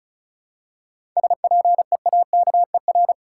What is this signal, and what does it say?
Morse code, heard as a keyed tone, about 34 wpm, 700 Hz. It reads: SPEAKER
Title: Seven Letter Words - 35wpm